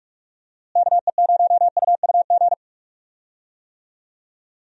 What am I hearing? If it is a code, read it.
Morse code, heard as a keyed tone, about 43 wpm, 700 Hz. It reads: KE0UUG
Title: USA Call Signs - 6 Characters - Repeated 2x - 45wpm